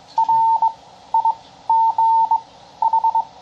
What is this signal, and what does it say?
Morse code, heard as a keyed tone, about 22 wpm, spaced as for 13 wpm, 900 Hz. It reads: RIGH
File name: bug sample